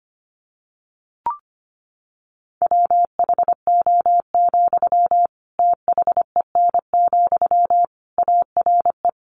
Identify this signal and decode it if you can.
Morse code, heard as a keyed tone, about 25 wpm, 700 Hz. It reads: WHO, THEN, ARE
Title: Sentences from Top 100 Words 25wpm